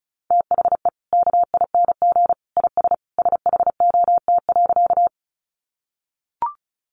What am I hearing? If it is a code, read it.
Morse code, heard as a keyed tone, about 34 wpm, 700 Hz. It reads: THE KING IS SHOT.
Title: Sentences from Top 1100 Words 35wpm